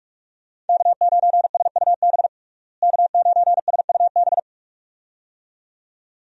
Morse code, heard as a keyed tone, about 46 wpm, 700 Hz. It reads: K9SUB K9SUB
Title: USA Call Signs - 5 Characters - Repeated 2x - 45wpm